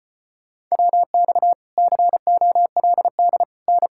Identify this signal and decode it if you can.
Morse code, heard as a keyed tone, about 38 wpm, 700 Hz. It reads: WX COLD N